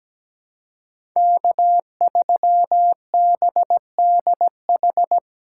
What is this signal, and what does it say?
Morse code, heard as a keyed tone, about 17 wpm, 700 Hz. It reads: K3BDH
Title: USA Call Signs - 5 Characters - 2x Character Spacing - 17wpm